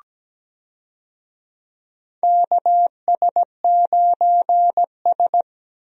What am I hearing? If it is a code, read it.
Morse code, heard as a keyed tone, about 17 wpm, 700 Hz. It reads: KS9S
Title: USA Call Signs - 4 Characters - 17wpm